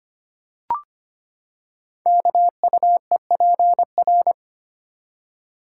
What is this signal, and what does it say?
Morse code, heard as a keyed tone, about 25 wpm, 700 Hz. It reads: KUEPR
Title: Code Groups - 5 Letters - 25wpm